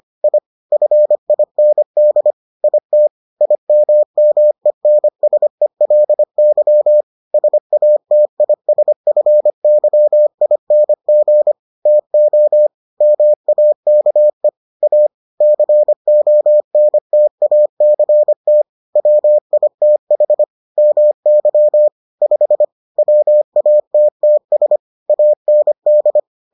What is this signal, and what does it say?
Morse code, heard as a keyed tone, about 25 wpm, 600 Hz. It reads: I FIND IT IMMENSELY SATISFYING TO MAKE A CONTACT WITH MY 5 WATTS AND